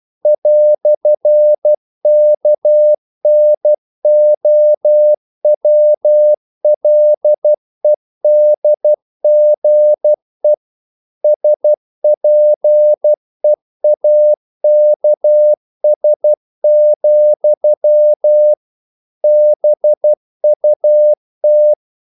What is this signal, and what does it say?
Morse code, heard as a keyed tone, about 12 wpm, 600 Hz. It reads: "KNOWLEDGE SPEAKS, BUT
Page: English quotations in CW (telegraphy - morse)